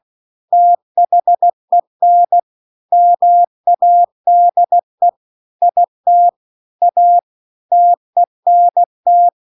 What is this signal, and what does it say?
Morse code, heard as a keyed tone, about 16 wpm, 700 Hz. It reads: THEN MADE IT A TENT